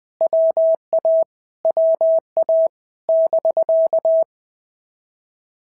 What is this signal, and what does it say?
Morse code, heard as a keyed tone, about 20 wpm, 650 Hz. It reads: WA WA <BK>
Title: POTA QSO Practice - 20wpm